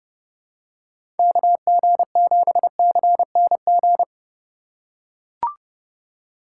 Morse code, heard as a keyed tone, about 30 wpm, 700 Hz. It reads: KG7CNG
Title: USA Call Signs - 2x Character Spacing - 30wpm